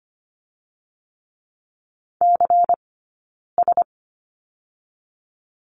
Morse code, heard as a keyed tone, about 25 wpm, spaced as for 9 wpm, 700 Hz. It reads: CS
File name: course / 25:10wpm / 099 Lesson 17d - Letter u - Two Characters 25wpm